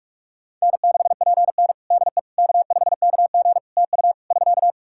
Morse code, heard as a keyed tone, about 48 wpm, 700 Hz. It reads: N6PN DE K5KG TU <SK>
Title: Practice QSO - Exchange 4 - Ending - 45wpm